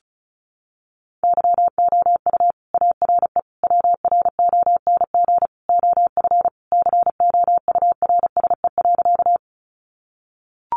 Morse code, heard as a keyed tone, about 34 wpm, 700 Hz. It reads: YOU ARE WRONG OF COURSE.
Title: Sentences from Top 900 Words 35wpm